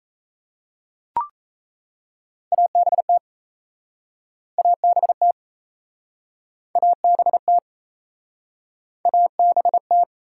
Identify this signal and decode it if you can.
Morse code, heard as a keyed tone, about 37 wpm, 700 Hz. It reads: ABT ABT ABT ABT